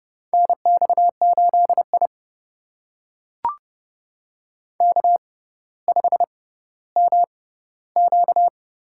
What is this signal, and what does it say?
Morse code, heard as a keyed tone, about 30 wpm, spaced as for 15 wpm, 700 Hz. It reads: NX8I K5MQ
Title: USA Call Signs - 4 Characters - 6x Character Spacing - 30wpm